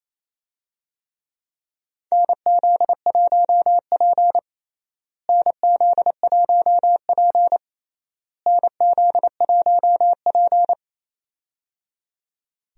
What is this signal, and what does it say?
Morse code, heard as a keyed tone, about 28 wpm, 700 Hz. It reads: NZ1P NZ1P NZ1P
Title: USA Call Signs - 4 Characters - Repeated 3x - 28wpm